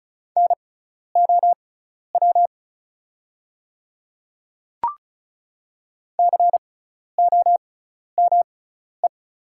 Morse code, heard as a keyed tone, about 34 wpm, spaced as for 12 wpm, 700 Hz. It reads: NOW COME
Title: 6x Character Spacing - Top 100 Words - 35wpm